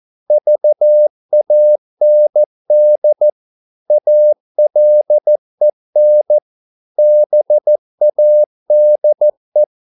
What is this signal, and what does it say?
Morse code, heard as a keyed tone, about 14 wpm, 600 Hz. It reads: VAND ALEN BADE
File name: Korte ord DK 14wpm